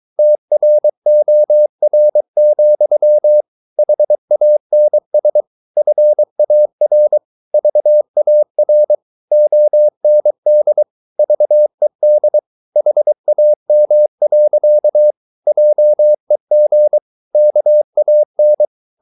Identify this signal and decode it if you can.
Morse code, heard as a keyed tone, about 23 wpm, 600 Hz. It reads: TROR, HANS FAR VAR OND VED HAM. JEG KAN
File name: Ferie QRS 22wpm